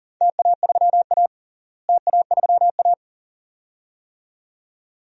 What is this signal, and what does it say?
Morse code, heard as a keyed tone, about 39 wpm, 700 Hz. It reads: TA3A TA3A
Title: DXCC-100 Call Signs - 4 Characters - Repeated 2x - 40wpm